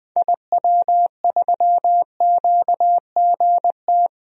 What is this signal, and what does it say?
Morse code, heard as a keyed tone, about 20 wpm, 700 Hz. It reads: IW3QGT
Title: DXCC-100 Call Signs - 6 Characters - 20wpm